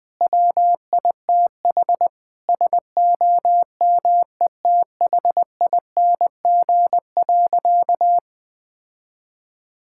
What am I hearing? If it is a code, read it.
Morse code, heard as a keyed tone, about 20 wpm, 700 Hz. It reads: WITH SOMETHING.
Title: All-Age Appropriate Jokes - 20wpm